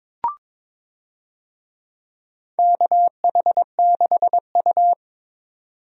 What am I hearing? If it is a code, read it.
Morse code, heard as a keyed tone, about 22 wpm, 700 Hz. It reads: KH6U
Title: DXCC-100 Call Signs - 4 Characters - 22wpm